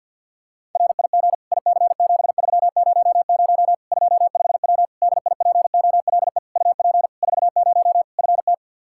Morse code, heard as a keyed tone, about 52 wpm, 700 Hz. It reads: RIG IC7300 25W DIPOLE UP 40 FT
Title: K1USN SST QSO Practice 50wpm